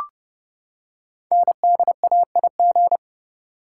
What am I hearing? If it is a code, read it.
Morse code, heard as a keyed tone, about 28 wpm, 700 Hz. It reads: NDAIG
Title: Code Groups - 5 Letters - 30wpm